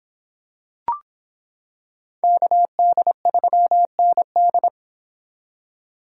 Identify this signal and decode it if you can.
Morse code, heard as a keyed tone, about 26 wpm, 700 Hz. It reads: KD3ND